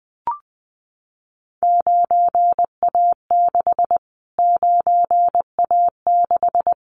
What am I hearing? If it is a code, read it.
Morse code, heard as a keyed tone, about 20 wpm, 700 Hz. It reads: 9A6 9A6
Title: Callsign Prefixes for Top 25 DXCC - Repeated 3x - 20wpm